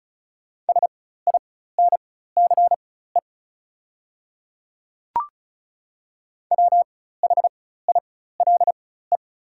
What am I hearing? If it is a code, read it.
Morse code, heard as a keyed tone, about 36 wpm, spaced as for 16 wpm, 690 Hz. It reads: SINCE WHILE
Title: 4x Character Spacing - Top 500 Words - 35wpm